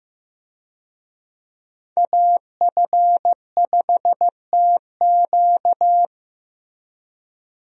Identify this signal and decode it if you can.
Morse code, heard as a keyed tone, about 15 wpm, 700 Hz. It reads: AF5TQ
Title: USA Call Signs - 5 Characters - Repeated 2x - 15wpm